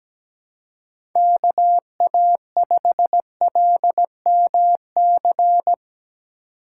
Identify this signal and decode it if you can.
Morse code, heard as a keyed tone, about 17 wpm, 700 Hz. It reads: KA5LMC
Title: USA Call Signs - 6 Characters - 8x Character Spacing - 17wpm